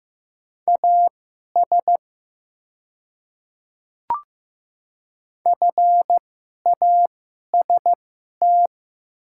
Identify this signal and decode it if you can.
Morse code, heard as a keyed tone, about 15 wpm, spaced as for 11 wpm, 690 Hz. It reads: AS FAST
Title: 2x Character Spacing - Top 500 Words - 15wpm